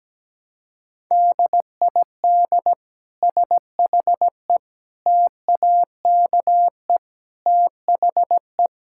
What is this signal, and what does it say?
Morse code, heard as a keyed tone, about 17 wpm, 700 Hz. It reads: DID SHE TAKE THE
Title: Sentences from Top 1000 Words - Review Part 6 - 17wpm